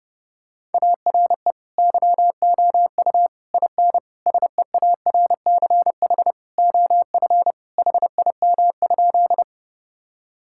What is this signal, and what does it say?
Morse code, heard as a keyed tone, about 32 wpm, 700 Hz. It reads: ARE YOU IN SEARCH OF HIM?
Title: Sentences from Top 1000 Words - Review Part 5 - 30wpm